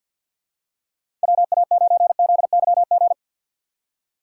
Morse code, heard as a keyed tone, about 53 wpm, 700 Hz. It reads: WA9ZYG
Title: USA Call Signs - 6 Characters - 6x Character Spacing - 50wpm